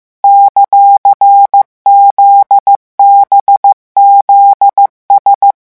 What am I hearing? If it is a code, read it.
Morse code, heard as a keyed tone, about 15 wpm, 800 Hz. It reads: ;ZBZS
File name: morse mp3 / revision17-vitesse 14 mots